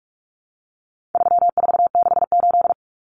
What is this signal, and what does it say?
Morse code, heard as a keyed tone, about 44 wpm, 700 Hz. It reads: 3468
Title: Numbers - 4-Digit 45wpm